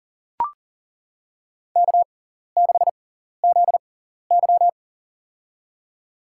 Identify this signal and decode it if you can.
Morse code, heard as a keyed tone, about 37 wpm, spaced as for 14 wpm, 700 Hz. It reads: K6ZY
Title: USA Call Signs - 4 Characters - 6x Character Spacing - 40wpm